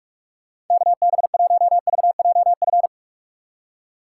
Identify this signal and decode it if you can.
Morse code, heard as a keyed tone, about 46 wpm, 700 Hz. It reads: KB1VJF
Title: USA Call Signs - 6 Characters - 4x Character Spacing - 45wpm